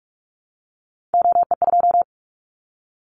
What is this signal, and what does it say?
Morse code, heard as a keyed tone, about 44 wpm, 700 Hz. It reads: OE2
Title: Callsign Prefixes for Top 100 DXCC - 45wpm